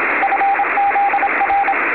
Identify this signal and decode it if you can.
Morse code, heard as a keyed tone, about 27 wpm, 800 Hz, with radio noise. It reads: FZR
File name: beacons / Frzr / fzrbcn